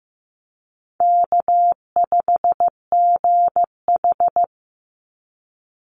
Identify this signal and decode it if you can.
Morse code, heard as a keyed tone, about 15 wpm, 700 Hz. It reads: K5GH
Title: Easy Call Signs - 15wpm